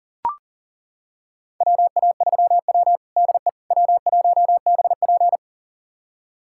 Morse code, heard as a keyed tone, about 40 wpm, 700 Hz. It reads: WA3W DE W1BP
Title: USA Callsign DE Callsign - 4 Characters - 40wpm